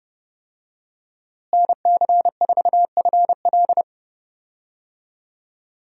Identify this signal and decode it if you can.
Morse code, heard as a keyed tone, about 30 wpm, 700 Hz. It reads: NC4FL
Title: USA Call Signs - 5 Characters - Repeated 2x - 30wpm